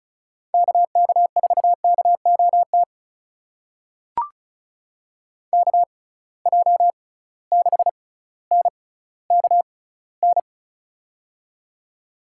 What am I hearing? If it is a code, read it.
Morse code, heard as a keyed tone, about 34 wpm, spaced as for 18 wpm, 700 Hz. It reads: KK4KOT KJ6NKN
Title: USA Call Signs - 6 Characters - 6x Character Spacing - 35wpm